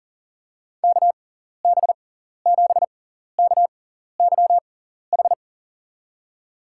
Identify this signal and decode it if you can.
Morse code, heard as a keyed tone, about 40 wpm, spaced as for 14 wpm, 700 Hz. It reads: KB7KYH
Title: USA Call Signs - 6x Character Spacing - 40wpm